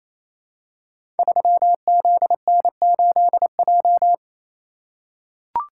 Morse code, heard as a keyed tone, about 28 wpm, 700 Hz. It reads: 3ZN8J
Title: Code Groups - 5 Letter:Numbers - 28wpm